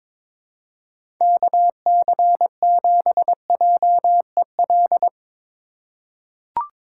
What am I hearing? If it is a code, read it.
Morse code, heard as a keyed tone, about 22 wpm, 700 Hz. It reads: KC7JEL
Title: USA Call Signs - 3x Character Spacing - 22wpm